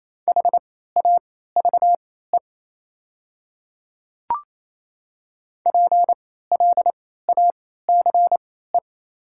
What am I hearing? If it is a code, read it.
Morse code, heard as a keyed tone, about 28 wpm, spaced as for 16 wpm, 690 Hz. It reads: HAVE PLACE
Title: 3x Character Spacing - Top 200 Words - 28wpm